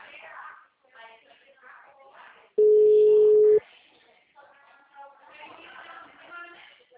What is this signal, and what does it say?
Ringback tone (Europe, 425 Hz), with speech.